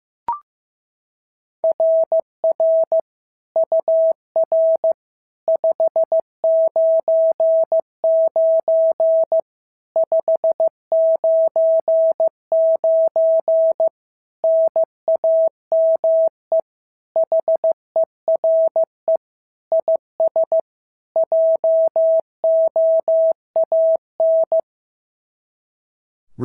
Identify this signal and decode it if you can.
Morse code, heard as a keyed tone, about 15 wpm, 650 Hz. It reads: RR UR 599 599 NAME HERE IS JOAN
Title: SOTA QSO Practice 15wpm